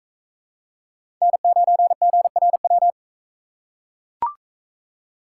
Morse code, heard as a keyed tone, about 42 wpm, 700 Hz. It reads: N9GRW